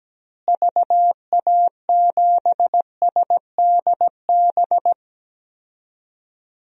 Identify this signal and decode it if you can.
Morse code, heard as a keyed tone, about 17 wpm, 700 Hz. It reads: VA7SDB
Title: DXCC-100 Call Signs - 6 Characters - Repeated 2x - 17wpm